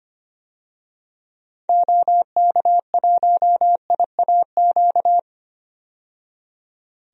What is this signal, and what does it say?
Morse code, heard as a keyed tone, about 25 wpm, 700 Hz. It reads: OK1IAQ
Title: DXCC-100 Call Signs - 6 Characters - 25wpm